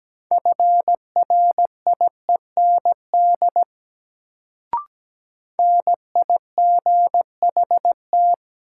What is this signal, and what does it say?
Morse code, heard as a keyed tone, about 17 wpm, 700 Hz. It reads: FRIEND NIGHT
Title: Sentences from Top 300 Words - New Word Drill - 17wpm